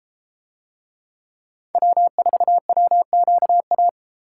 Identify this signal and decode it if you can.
Morse code, heard as a keyed tone, about 32 wpm, 700 Hz. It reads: W4WQA